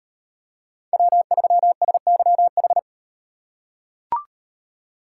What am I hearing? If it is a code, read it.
Morse code, heard as a keyed tone, about 38 wpm, 700 Hz. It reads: W3SYH